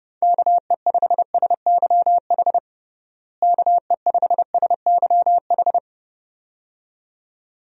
Morse code, heard as a keyed tone, about 32 wpm, 700 Hz. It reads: KE5SYH KE5SYH
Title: USA Call Signs - 6 Characters - Repeated 2x - 30wpm